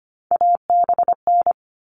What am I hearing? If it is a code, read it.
Morse code, heard as a keyed tone, about 25 wpm, 700 Hz. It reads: ABN
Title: All Cut Numbers - 3-Digit 25wpm